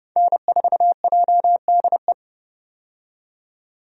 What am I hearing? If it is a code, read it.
Morse code, heard as a keyed tone, about 29 wpm, 700 Hz. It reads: N4JDE
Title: USA Call Signs - 5 Characters - 30wpm